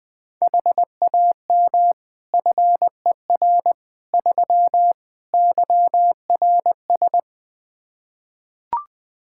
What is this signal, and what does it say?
Morse code, heard as a keyed tone, about 20 wpm, 700 Hz. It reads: HAM FER 3 YRS
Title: Practice QSO - Exchange 3 - More Info - 20wpm